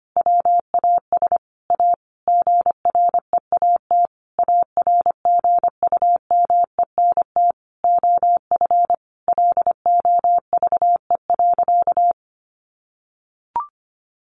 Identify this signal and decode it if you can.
Morse code, heard as a keyed tone, about 25 wpm, 700 Hz. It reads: WAS A GREAT ARGUMENT OF LOVE.
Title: Sentences from Top 1400 Words 25wpm